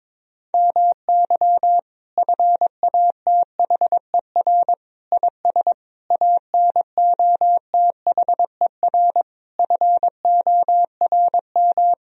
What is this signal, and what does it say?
Morse code, heard as a keyed tone, about 22 wpm, 700 Hz. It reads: MY FATHER IS ANOTHER FORM
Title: Sentences from Top 1000 Words - Review Part 1 - 22wpm